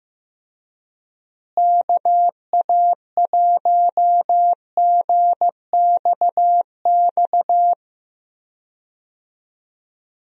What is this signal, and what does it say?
Morse code, heard as a keyed tone, about 15 wpm, 700 Hz. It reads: KA1GXX
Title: USA Call Signs - 6 Characters - Repeated 2x - 15wpm